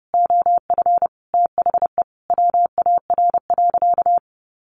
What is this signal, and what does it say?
Morse code, heard as a keyed tone, about 28 wpm, 700 Hz. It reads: OF THE WAR.
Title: Sentences from Top 1400 Words 30wpm